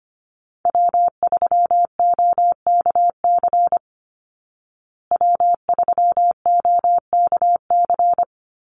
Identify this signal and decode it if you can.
Morse code, heard as a keyed tone, about 25 wpm, 700 Hz. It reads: W3OKC W3OKC
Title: Call Signs Repeated 3x - 25wpm